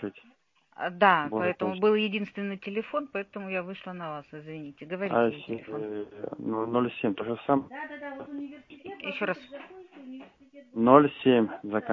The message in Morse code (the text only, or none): none